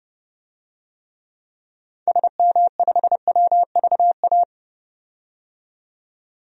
Morse code, heard as a keyed tone, about 30 wpm, 700 Hz. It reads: SM5WVA
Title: DXCC-100 Call Signs - 6 Characters - 30wpm